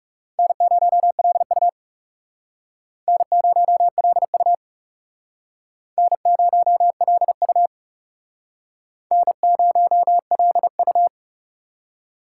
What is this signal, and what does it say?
Morse code, heard as a keyed tone, about 40 wpm, 700 Hz. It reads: N0LU N0LU N0LU N0LU